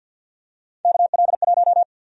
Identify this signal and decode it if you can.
Morse code, heard as a keyed tone, about 46 wpm, 700 Hz. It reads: KL1
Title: USA Call Sign Prefixes - 50wpm